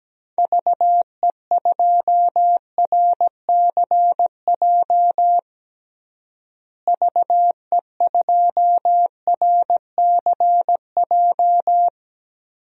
Morse code, heard as a keyed tone, about 17 wpm, 700 Hz. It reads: VE2RCJ VE2RCJ
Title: DXCC-100 Call Signs - 6 Characters - Repeated 2x - 17wpm